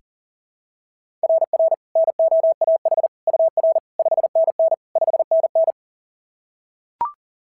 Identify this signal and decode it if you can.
Morse code, heard as a keyed tone, about 40 wpm, 690 Hz. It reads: RR NOAH UR 5NN 5NN
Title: SOTA QSO Practice 40wpm